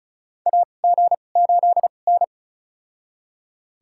Morse code, heard as a keyed tone, about 34 wpm, spaced as for 25 wpm, 700 Hz. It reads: AG8N
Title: USA Call Signs - 4 Characters - 2x Character Spacing - 35wpm